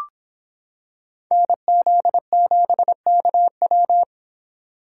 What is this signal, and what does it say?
Morse code, heard as a keyed tone, about 26 wpm, 700 Hz. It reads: NZ7KW